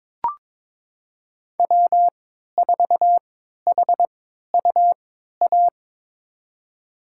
Morse code, heard as a keyed tone, about 22 wpm, spaced as for 13 wpm, 700 Hz. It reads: W4HUA
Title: USA Call Signs - 3x Character Spacing - 22wpm